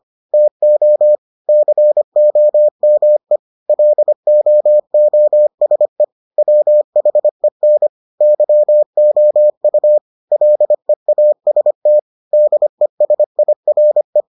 Morse code, heard as a keyed tone, about 25 wpm, 600 Hz. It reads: TO COME LOOSE WHEN YOU LEAST DESIRE